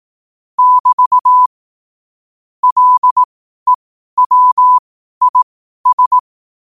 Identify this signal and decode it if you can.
Morse code, heard as a keyed tone, about 18 wpm, spaced as for 12 wpm, 1000 Hz. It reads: <BT> LEWIS
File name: qotdfiles / 12wpm